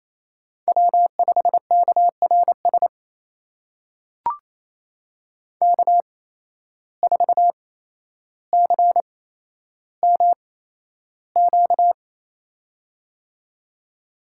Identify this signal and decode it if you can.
Morse code, heard as a keyed tone, about 27 wpm, spaced as for 12 wpm, 700 Hz. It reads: W5KRS K4CMQ
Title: USA Call Signs - 5 Characters - 8x Character Spacing - 28wpm